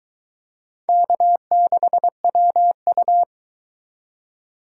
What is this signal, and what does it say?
Morse code, heard as a keyed tone, about 22 wpm, 700 Hz. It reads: K6WU